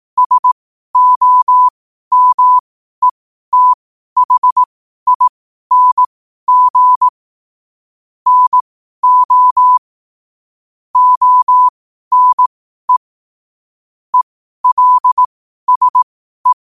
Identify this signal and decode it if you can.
Morse code, heard as a keyed tone, about 18 wpm, spaced as for 12 wpm, 1000 Hz. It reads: SOMETHING NO ONE ELSE
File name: qotdfiles / 12wpm